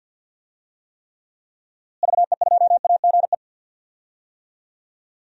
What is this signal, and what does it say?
Morse code, heard as a keyed tone, about 51 wpm, 700 Hz. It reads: VE2AGE
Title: DXCC-100 Call Signs - 6 Characters - Repeated 3x - 50wpm